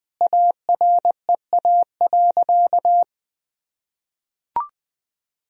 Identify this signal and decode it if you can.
Morse code, heard as a keyed tone, about 21 wpm, 700 Hz. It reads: AREA.
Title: Sentences from Top 1000 Words - Review Part 2 - 20wpm